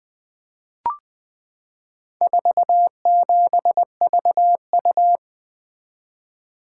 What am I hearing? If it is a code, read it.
Morse code, heard as a keyed tone, about 21 wpm, 700 Hz. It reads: 47VU
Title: Four Letter-Number Combinations - 20wpm